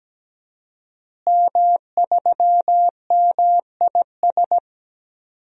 Morse code, heard as a keyed tone, about 17 wpm, 700 Hz. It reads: M3MIS
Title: DXCC-100 Call Signs - 5 Characters - Repeated 2x - 17wpm